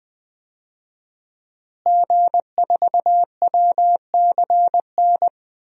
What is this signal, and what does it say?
Morse code, heard as a keyed tone, about 21 wpm, 700 Hz. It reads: G4WCN
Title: DXCC-100 Call Signs - 5 Characters - Repeated 2x - 20wpm